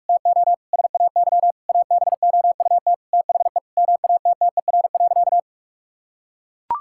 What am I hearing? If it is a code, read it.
Morse code, heard as a keyed tone, about 41 wpm, 700 Hz. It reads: TO SAY ABOUT THE MATTER.
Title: Sentences from Top 1000 Words - Review Part 5 - 45wpm